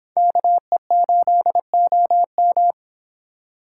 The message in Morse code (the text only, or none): KE8OM